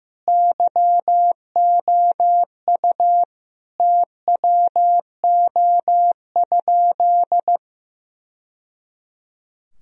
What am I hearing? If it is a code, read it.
Morse code, heard as a keyed tone, about 16 wpm, 700 Hz. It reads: YOU TWO?
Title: Sentences from Top 1000 Words - Review Part 1 - 15wpm